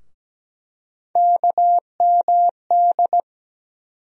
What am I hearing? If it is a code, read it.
Morse code, heard as a keyed tone, about 17 wpm, 700 Hz. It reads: KMD
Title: USA Call Sign Postfixes - 3 Letters 17wpm